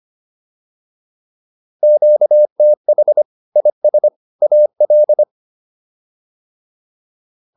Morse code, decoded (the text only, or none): QTH IS AL